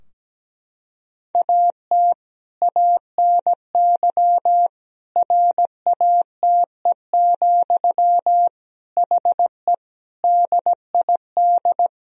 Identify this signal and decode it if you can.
Morse code, heard as a keyed tone, about 17 wpm, 700 Hz. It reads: AT ANY RATE, HE DID
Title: Sentences from Top 1000 Words - Review Part 4 - 17wpm